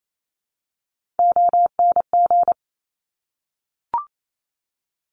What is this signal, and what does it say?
Morse code, heard as a keyed tone, about 28 wpm, 700 Hz. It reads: ONG
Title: Top Three Letter Combinations in Words 28wpm